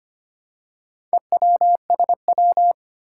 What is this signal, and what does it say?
Morse code, heard as a keyed tone, about 25 wpm, 700 Hz. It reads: EWSW
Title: Four Challenging Letter-Number Combinations 25wpm